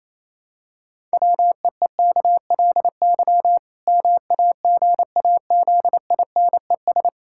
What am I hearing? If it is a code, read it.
Morse code, heard as a keyed tone, about 28 wpm, 700 Hz. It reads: WEEKLY MAGAZINES
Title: Pangram Challenge v2 - 28wpm